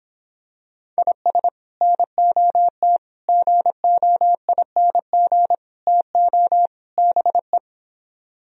Morse code, heard as a keyed tone, about 26 wpm, 700 Hz. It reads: IS NOT GOING TO BE